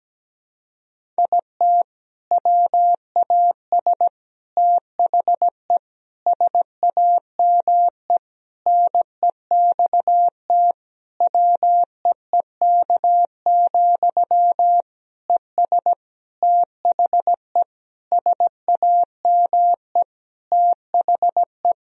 Morse code, heard as a keyed tone, about 17 wpm, 700 Hz. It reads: IT WAS THE SAME NEXT WEEK, ES THE SAME THE
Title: Sentences from Top 500 Words and Abbreviations 17wpm